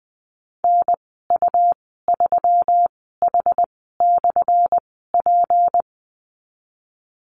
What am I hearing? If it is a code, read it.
Morse code, heard as a keyed tone, about 20 wpm, spaced as for 15 wpm, 700 Hz. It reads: NU3H/P
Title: 179 Lesson 29g - Slash - Call Signs 20wpm